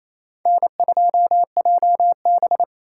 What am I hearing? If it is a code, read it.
Morse code, heard as a keyed tone, about 28 wpm, 700 Hz. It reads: N2JB
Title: USA Call Signs - 3x Character Spacing - 28wpm